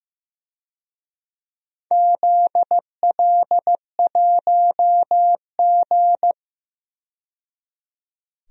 Morse code, heard as a keyed tone, about 15 wpm, 700 Hz. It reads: ZL1G
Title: DXCC-100 Call Signs - 4 Characters - 15wpm